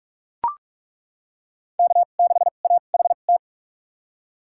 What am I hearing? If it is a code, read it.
Morse code, heard as a keyed tone, about 42 wpm, spaced as for 31 wpm, 700 Hz. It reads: K6AHT
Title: USA Call Signs - 5 Characters - 2x Character Spacing - 45wpm